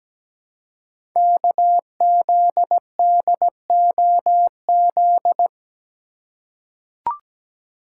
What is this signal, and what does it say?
Morse code, heard as a keyed tone, about 17 wpm, 700 Hz. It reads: KZDOZ
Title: Code Groups - 5 Letter:Numbers - 17wpm